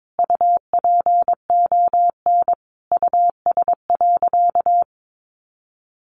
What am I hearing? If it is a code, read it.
Morse code, heard as a keyed tone, about 22 wpm, 700 Hz. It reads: UPON US.
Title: Sentences from Top 1400 Words 22wpm